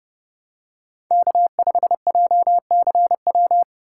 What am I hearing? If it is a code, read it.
Morse code, heard as a keyed tone, about 30 wpm, 700 Hz. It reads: K5JCW